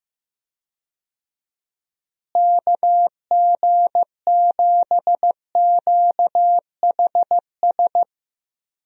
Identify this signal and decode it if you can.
Morse code, heard as a keyed tone, about 14 wpm, 700 Hz. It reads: KG7QHS
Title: USA Call Signs - 6 Characters - Repeated 2x - 15wpm